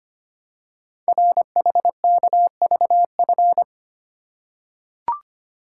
Morse code, heard as a keyed tone, about 25 wpm, 700 Hz. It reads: RHKVF
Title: Code Groups - 5 Letters - 25wpm